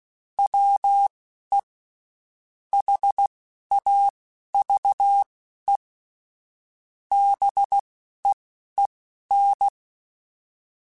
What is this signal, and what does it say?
Morse code, heard as a keyed tone, about 16 wpm, spaced as for 11 wpm, 800 Hz. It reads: WE HAVE BEEN